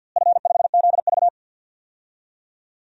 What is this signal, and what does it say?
Morse code, heard as a keyed tone, about 47 wpm, 700 Hz. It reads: F5ZV
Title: DXCC-100 Call Signs - 4 Characters - Repeated 2x - 50wpm